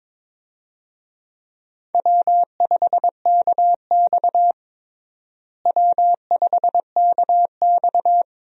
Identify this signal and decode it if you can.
Morse code, heard as a keyed tone, about 22 wpm, 700 Hz. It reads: W5KX W5KX
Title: USA Call Signs - 4 Characters - Repeated 3x - 22wpm